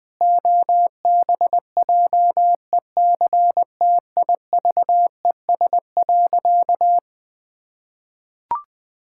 Morse code, heard as a keyed tone, about 20 wpm, 700 Hz. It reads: OBJECTIVES.
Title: Pangram Challenge v2 - 20wpm